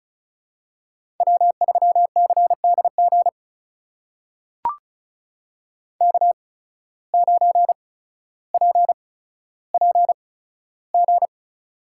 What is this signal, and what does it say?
Morse code, heard as a keyed tone, about 34 wpm, spaced as for 15 wpm, 690 Hz. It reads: W3CDG K9PPG
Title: USA Call Signs - 5 Characters - 8x Character Spacing - 35wpm